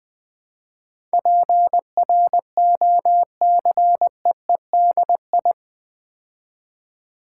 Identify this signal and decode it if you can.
Morse code, heard as a keyed tone, about 20 wpm, 700 Hz. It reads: PROCEEDI
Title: Build A Word - 4 Letters At A Time - 10-Letter Words - 20wpm